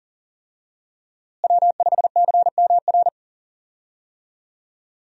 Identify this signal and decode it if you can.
Morse code, heard as a keyed tone, about 40 wpm, 700 Hz. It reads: W5CMR
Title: USA Call Signs - 5 Characters - Repeated 3x - 40wpm